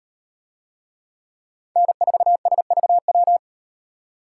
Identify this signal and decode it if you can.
Morse code, heard as a keyed tone, about 36 wpm, 700 Hz. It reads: N4SVW